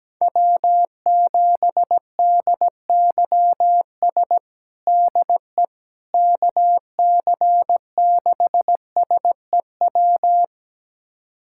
Morse code, heard as a keyed tone, about 17 wpm, 700 Hz. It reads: W7DYS DE KC6SEW
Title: USA Callsign DE Callsign 17wpm